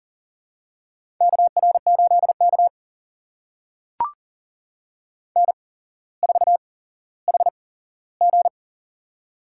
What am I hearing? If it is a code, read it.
Morse code, heard as a keyed tone, about 39 wpm, spaced as for 17 wpm, 700 Hz. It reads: KR8K N4HG
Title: USA Call Signs - 4 Characters - 8x Character Spacing - 40wpm